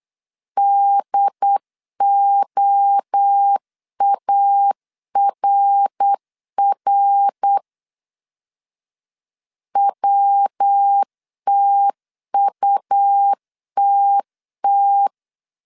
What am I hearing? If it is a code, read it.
Morse code, heard as a keyed tone, about 8 wpm, 800 Hz. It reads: DOARR WTUTT
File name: CW / DET / 1 3 1500C S C 8wpm 4214